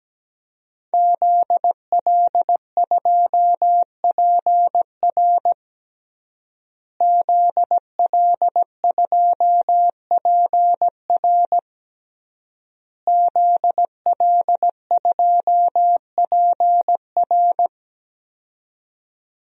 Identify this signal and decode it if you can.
Morse code, heard as a keyed tone, about 17 wpm, 700 Hz. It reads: ZL2PR ZL2PR ZL2PR
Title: DXCC-100 Call Signs - 5 Characters - Repeated 3x - 17wpm